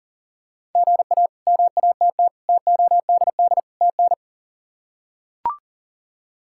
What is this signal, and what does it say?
Morse code, heard as a keyed tone, about 37 wpm, 700 Hz. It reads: GA MATT TODD TN
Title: K1USN SST QSO Practice - 40wpm